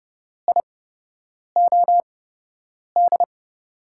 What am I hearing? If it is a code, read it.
Morse code, heard as a keyed tone, about 29 wpm, spaced as for 8 wpm, 700 Hz. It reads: IOD
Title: USA Call Signs - 5 Characters - 8x Character Spacing - 30wpm